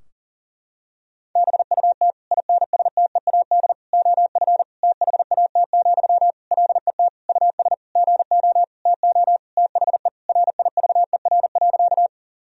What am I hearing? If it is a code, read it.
Morse code, heard as a keyed tone, about 40 wpm, 700 Hz. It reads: BUT INSTEAD OF THAT, LET US GO TO THE RIVER.
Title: Sentences from Top 1000 Words - Review Part 4 - 40wpm